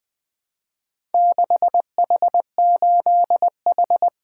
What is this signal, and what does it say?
Morse code, heard as a keyed tone, about 20 wpm, 700 Hz. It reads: 6H8H
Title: Four Letter-Number Combinations - 20wpm